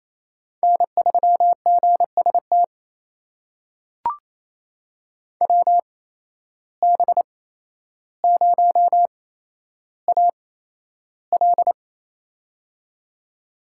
Morse code, heard as a keyed tone, about 28 wpm, spaced as for 12 wpm, 700 Hz. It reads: N3GST WB0AL
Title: USA Call Signs - 5 Characters - 8x Character Spacing - 28wpm